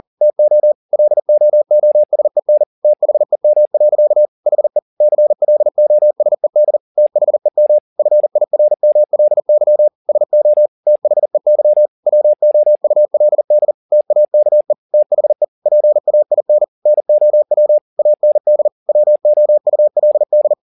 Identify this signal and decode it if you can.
Morse code, heard as a keyed tone, about 40 wpm, 600 Hz. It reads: TO LOOSEN THEM. HE CLOSED THEM FIRMLY SO THEY WOULD TAKE THE PAIN NOW AND WOULD
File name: the old man and the sea 2 40wpm